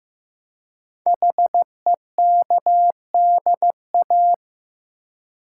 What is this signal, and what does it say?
Morse code, heard as a keyed tone, about 15 wpm, 700 Hz. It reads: HEKDA
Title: Code Groups - 5 Letters - 15wpm